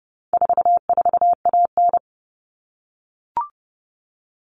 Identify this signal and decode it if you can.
Morse code, heard as a keyed tone, about 30 wpm, 700 Hz. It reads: 44AN
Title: All Cut Numbers - 4-Digit 30wpm